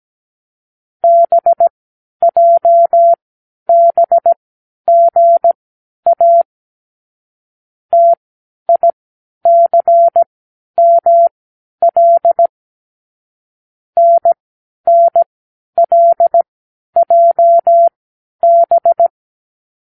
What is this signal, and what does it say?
Morse code, heard as a keyed tone, about 17 wpm, spaced as for 10 wpm, 680 Hz. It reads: BJBGA TICML NNLJB
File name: k7qo / Lesson 36 (A-X)